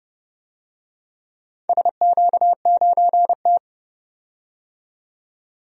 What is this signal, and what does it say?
Morse code, heard as a keyed tone, about 31 wpm, 700 Hz. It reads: SQ9T
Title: DXCC-100 Call Signs - 4 Characters - 30wpm